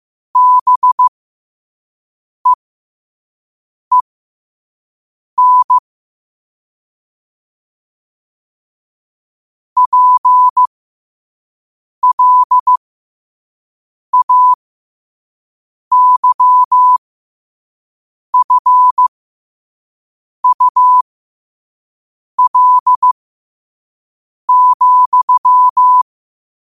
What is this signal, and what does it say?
Morse code, heard as a keyed tone, about 15 wpm, spaced as for 5 wpm, 1000 Hz. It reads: BEEN PLAYFUL,
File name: qotdfiles / 5wpm